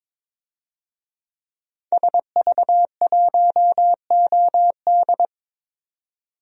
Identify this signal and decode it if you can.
Morse code, heard as a keyed tone, about 22 wpm, 700 Hz. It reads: SV1OD
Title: DXCC-100 Call Signs - 5 Characters - 22wpm